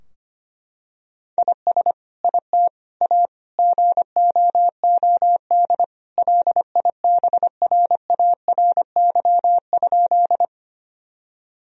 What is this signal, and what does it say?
Morse code, heard as a keyed tone, about 25 wpm, 700 Hz. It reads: IS IT A GOOD LIBRARY?
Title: Sentences from Top 1000 Words - Review Part 4 - 25wpm